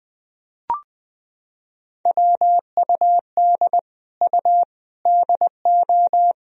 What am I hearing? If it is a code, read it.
Morse code, heard as a keyed tone, about 20 wpm, 700 Hz. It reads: WUD U DO
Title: Sentences from Top 100 Words and Abbreviations 20wpm